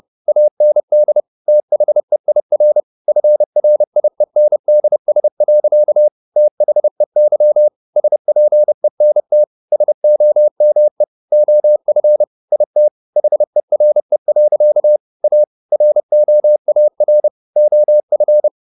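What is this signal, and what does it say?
Morse code, heard as a keyed tone, about 30 wpm, 600 Hz. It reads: AND THEIR FRIENDS. THEY SPENT SOME OF IT HERE. A ROAR OF
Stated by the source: Morse code at 30 wpm, the novel Never by Ken Follett in English, from CW med Gnister.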